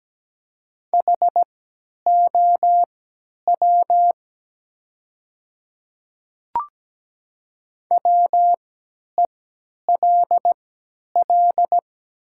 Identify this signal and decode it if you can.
Morse code, heard as a keyed tone, about 17 wpm, spaced as for 10 wpm, 700 Hz. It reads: HOW WELL
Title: 3x Character Spacing - Top 200 Words - 17wpm